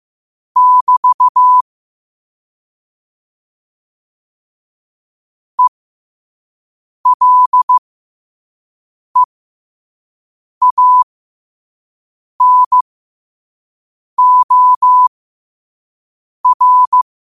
Morse code, heard as a keyed tone, about 15 wpm, spaced as for 5 wpm, 1000 Hz. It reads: <BT> ELEANOR
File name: qotdfiles / 5wpm